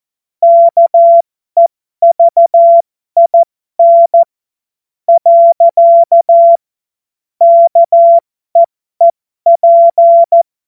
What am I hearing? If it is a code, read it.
Morse code, heard as a keyed tone, about 14 wpm, spaced as for 11 wpm, 680 Hz. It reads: KEVIN. KEEP
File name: Morsekurs / Witze